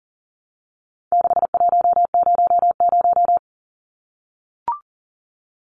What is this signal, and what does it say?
Morse code, heard as a keyed tone, about 38 wpm, 700 Hz. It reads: 6100
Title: Numbers - 4-Digit 40wpm